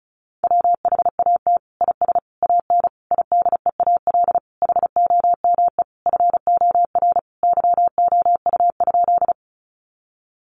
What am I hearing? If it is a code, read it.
Morse code, heard as a keyed tone, about 34 wpm, 700 Hz. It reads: WHAT IS AN IDEAL HOME FOR YOU?
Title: Sentences from Top 1400 Words 35wpm